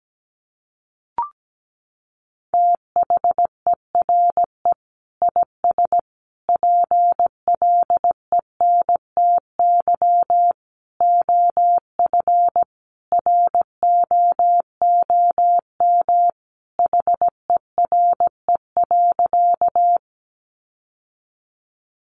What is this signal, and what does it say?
Morse code, heard as a keyed tone, about 17 wpm, 700 Hz. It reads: THERE IS PLENTY OF ROOM HERE.
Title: Sentences from Top 1300 Words 17wpm